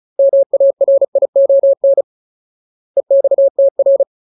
Morse code, heard as a keyed tone, about 34 wpm, 550 Hz. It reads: MARION EXTR
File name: Ultimate Word List / 35WPM Ultimate